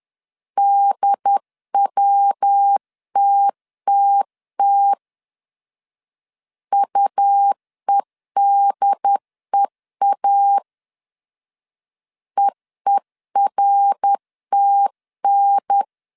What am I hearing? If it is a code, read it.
Morse code, heard as a keyed tone, about 11 wpm, 800 Hz. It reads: DWTTT UEDEA EERTN
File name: 10wpm